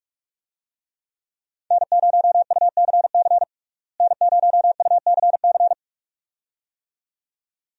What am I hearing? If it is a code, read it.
Morse code, heard as a keyed tone, about 41 wpm, 700 Hz. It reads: N0UCC N0UCC
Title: USA Call Signs - 5 Characters - Repeated 2x - 45wpm